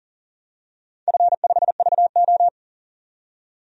Morse code, heard as a keyed tone, about 40 wpm, 700 Hz. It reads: F5VO
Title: DXCC-100 Call Signs - 4 Characters - 40wpm